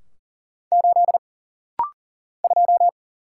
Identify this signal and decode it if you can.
Morse code, heard as a keyed tone, about 40 wpm, 700 Hz. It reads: 8 2
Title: Single Letter-Number - Rapid-Fire 40wpm